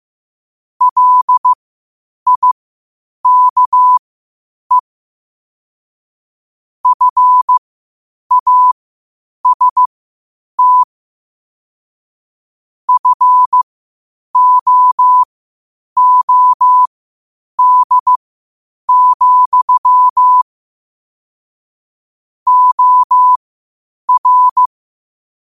Morse code, heard as a keyed tone, about 15 wpm, spaced as for 8 wpm, 1000 Hz. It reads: LIKE FAST FOOD, OR
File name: qotdfiles / 8wpm